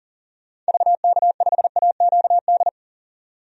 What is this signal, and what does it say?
Morse code, heard as a keyed tone, about 40 wpm, 700 Hz. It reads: VK5AQD
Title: DXCC-100 Call Signs - 6 Characters - 40wpm